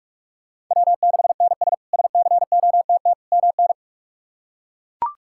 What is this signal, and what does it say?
Morse code, heard as a keyed tone, about 41 wpm, 700 Hz. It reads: W6NS SCOTT MN
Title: K1USN SST QSO Practice - 45wpm